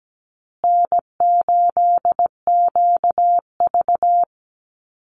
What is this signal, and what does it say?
Morse code, heard as a keyed tone, about 17 wpm, 700 Hz. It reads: N8QV
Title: USA Call Signs Repeated 2x 17wpm